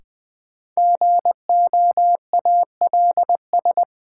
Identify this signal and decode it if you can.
Morse code, heard as a keyed tone, about 21 wpm, 700 Hz. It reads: GOALS
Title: Sentences from Top 1000 Words - Review Part 6 - New Word Drill - 20wpm